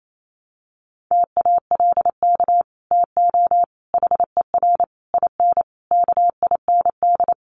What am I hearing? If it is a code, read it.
Morse code, heard as a keyed tone, about 28 wpm, 700 Hz. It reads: TALK TO HER IN KIND
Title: Sentences from Top 1500 Words 28wpm